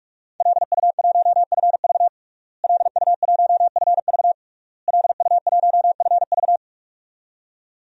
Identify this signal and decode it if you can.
Morse code, heard as a keyed tone, about 46 wpm, 700 Hz. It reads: LU1FV LU1FV LU1FV
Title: DXCC-100 Call Signs - 5 Characters - Repeated 3x - 45wpm